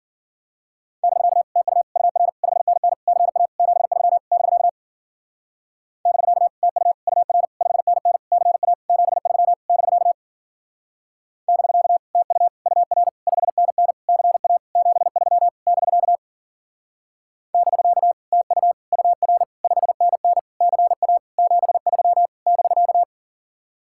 Morse code, heard as a keyed tone, about 51 wpm, 700 Hz. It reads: <BK> TU UR 5NN CA 73 <BK> <BK> TU UR 5NN CA 73 <BK> <BK> TU UR 5NN CA 73 <BK> <BK> TU UR 5NN CA 73 <BK>